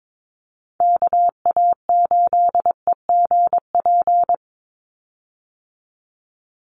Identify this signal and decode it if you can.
Morse code, heard as a keyed tone, about 22 wpm, 700 Hz. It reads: KA8EGP
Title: USA Call Signs 22wpm